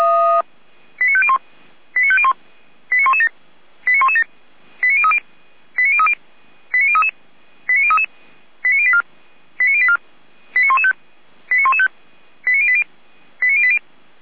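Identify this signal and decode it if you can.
Five-tone selective call (ZVEI), heard as quick runs of five tones, tones of about 70 ms. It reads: 89501 89501 89117 89117 89030 89030 89033 89033 89084 89084 89115 89115 89080 89080
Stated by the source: ZVEI five-tone selective call received on an SDR.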